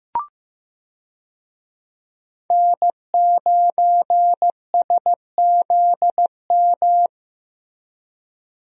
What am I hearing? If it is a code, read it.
Morse code, heard as a keyed tone, about 15 wpm, 700 Hz. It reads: N9SZM
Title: USA Call Signs - 5 Characters - Repeated 3x - 15wpm